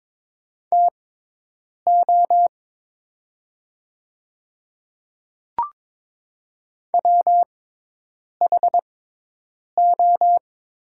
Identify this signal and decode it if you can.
Morse code, heard as a keyed tone, about 22 wpm, spaced as for 8 wpm, 700 Hz. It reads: TO WHO
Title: 6x Character Spacing - Top 300 Words - 22wpm